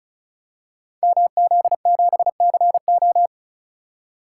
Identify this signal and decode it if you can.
Morse code, heard as a keyed tone, about 35 wpm, 700 Hz. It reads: MZ7CO
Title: Code Groups - 5 Letter:Numbers - 35wpm